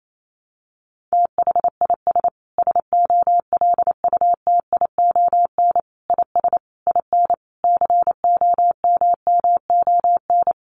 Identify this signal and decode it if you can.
Morse code, heard as a keyed tone, about 28 wpm, 700 Hz. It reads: THIS SOLUTION IS IN COMMON
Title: Sentences from Top 1300 Words 28wpm